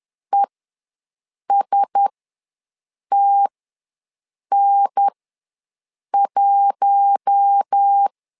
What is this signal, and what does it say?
Morse code, heard as a keyed tone, about 10 wpm, spaced as for 6 wpm, 800 Hz. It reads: ESTN1
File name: LEZIONI / 4 Q1S ws 10wpm 2208